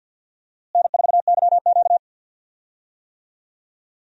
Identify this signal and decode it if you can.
Morse code, heard as a keyed tone, about 52 wpm, 700 Hz. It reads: N4YQ
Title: USA Call Signs - 4 Characters - Repeated 2x - 50wpm